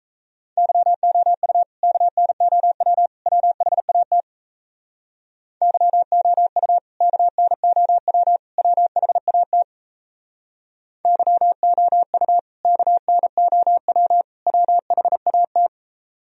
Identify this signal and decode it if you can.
Morse code, heard as a keyed tone, about 37 wpm, 700 Hz. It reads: YOU KNOW WHAT YOU KNOW WHAT YOU KNOW WHAT